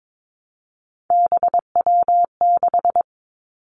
Morse code, heard as a keyed tone, about 22 wpm, 700 Hz. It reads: BW6
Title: Three Challenging Letter-Number Combinations 22wpm